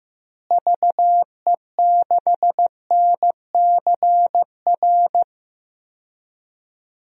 Morse code, heard as a keyed tone, about 15 wpm, 700 Hz. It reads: VE6NCR
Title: DXCC-100 Call Signs - 6 Characters - Repeated 3x - 15wpm